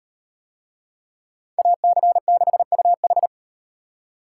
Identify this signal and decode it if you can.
Morse code, heard as a keyed tone, about 38 wpm, 700 Hz. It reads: AC6UH